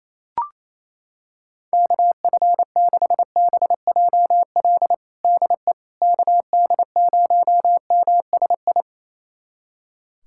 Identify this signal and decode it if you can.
Morse code, heard as a keyed tone, about 28 wpm, 700 Hz. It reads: KF6BJL DE KD0MSI
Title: USA Callsign DE Callsign - 6 Characters - 28wpm